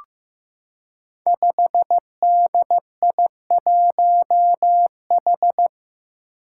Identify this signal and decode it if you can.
Morse code, heard as a keyed tone, about 15 wpm, 700 Hz. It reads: 5DI1H
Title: Five Challenging Letter-Number Combinations 15wpm